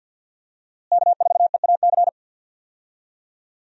Morse code, heard as a keyed tone, about 52 wpm, 700 Hz. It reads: K4EAC
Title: USA Call Signs - 5 Characters - Repeated 3x - 50wpm